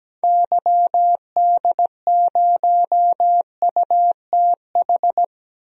Morse code, heard as a keyed tone, about 17 wpm, 700 Hz. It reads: YD0UTH
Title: DXCC-100 Call Signs - 6 Characters - 17wpm